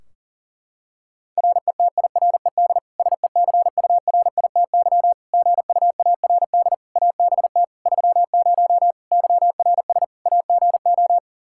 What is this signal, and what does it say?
Morse code, heard as a keyed tone, about 40 wpm, 700 Hz. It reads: RETIRED SECURITY GUARD ABT 30 YRS AGO
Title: K1USN SST QSO Practice 40wpm